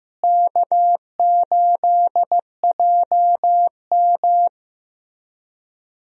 Morse code, heard as a keyed tone, about 15 wpm, 700 Hz. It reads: K8JM
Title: USA Call Signs - 4 Characters - Repeated 3x - 15wpm